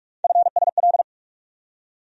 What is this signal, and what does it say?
Morse code, heard as a keyed tone, about 46 wpm, 700 Hz. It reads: FSL
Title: USA Call Sign Postfixes - 3 Letters 45wpm